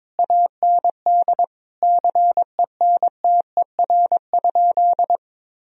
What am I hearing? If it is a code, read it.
Morse code, heard as a keyed tone, about 22 wpm, 700 Hz. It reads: AND CENTER?
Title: Sentences from Top 1000 Words - Review Part 2 - 22wpm